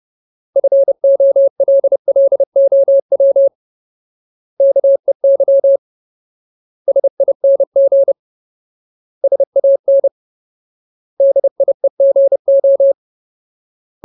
Morse code, heard as a keyed tone, about 30 wpm, 550 Hz. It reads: FOLLOW KEY SING SAN DIEGO
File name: Ultimate Word List / 30WPM Ultimate